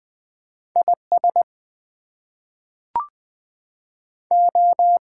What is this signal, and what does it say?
Morse code, heard as a keyed tone, about 20 wpm, 700 Hz. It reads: IS O
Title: Two Letter Words - 20wpm